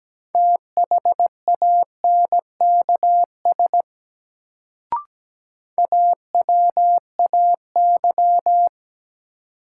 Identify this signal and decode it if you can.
Morse code, heard as a keyed tone, about 17 wpm, 700 Hz. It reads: THANKS AWAY
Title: Sentences from Top 600 Words - New Word Drill - 17wpm